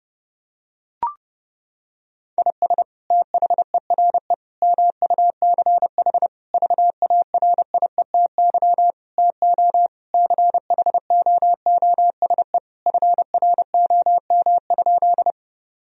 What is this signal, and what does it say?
Morse code, heard as a keyed tone, about 30 wpm, 700 Hz. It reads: IS THERE MUCH VARIETY TO CHOOSE FROM?
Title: Sentences from Top 1000 Words - Review Part 4 - 30wpm